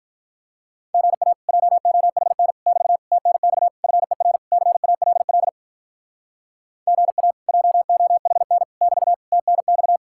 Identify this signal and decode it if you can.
Morse code, heard as a keyed tone, about 45 wpm, 700 Hz. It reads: GA JOHN <BT> TNX FER CALL GA JOHN <BT> TNX